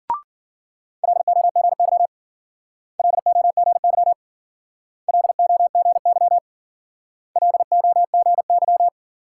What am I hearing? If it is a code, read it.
Morse code, heard as a keyed tone, about 51 wpm, 700 Hz. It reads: LOGY LOGY LOGY LOGY